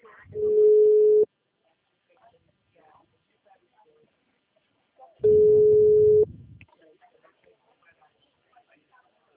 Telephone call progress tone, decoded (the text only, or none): ringback tone